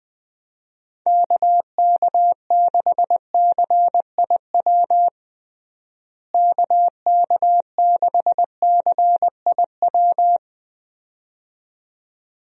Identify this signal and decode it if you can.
Morse code, heard as a keyed tone, about 21 wpm, 700 Hz. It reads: KK6CIW KK6CIW
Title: USA Call Signs - 6 Characters - Repeated 2x - 20wpm